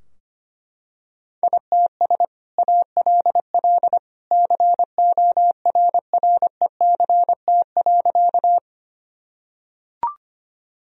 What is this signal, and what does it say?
Morse code, heard as a keyed tone, about 25 wpm, 700 Hz. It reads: ITS ALL CORRECT.
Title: Sentences from Top 1000 Words - Review Part 2 - 25wpm